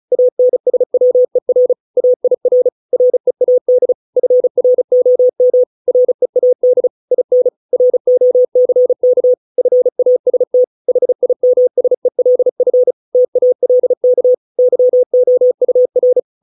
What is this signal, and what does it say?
Morse code, heard as a keyed tone, about 34 wpm, 500 Hz. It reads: ANSWER AIR READ FROM READ IN ROCK FAST HIMSELF TALK YOUR